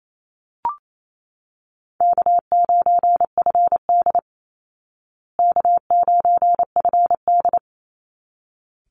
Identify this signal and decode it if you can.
Morse code, heard as a keyed tone, about 28 wpm, 700 Hz. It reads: K9FD K9FD
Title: Call Signs Repeated 3x - 28wpm